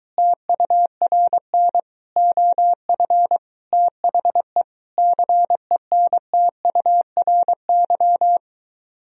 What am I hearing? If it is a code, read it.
Morse code, heard as a keyed tone, about 23 wpm, 700 Hz. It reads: TURN OF THE CENTURY